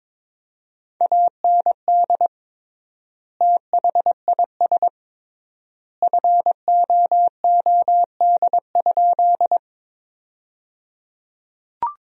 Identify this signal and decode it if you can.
Morse code, heard as a keyed tone, about 21 wpm, 700 Hz. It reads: AND THIS FOOD?
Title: 3x Word-Spacing - Sentences from Top 300 Words - 22wpm